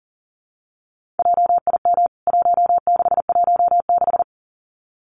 Morse code, heard as a keyed tone, about 38 wpm, 700 Hz. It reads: JIM 1616
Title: CWT Exchange 40wpm